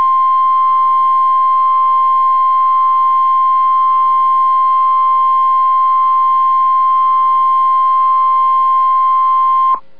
Emergency Alert System (SAME) tones (the 1050 Hz attention tone). It is heard from 0.6 to 9.7 s.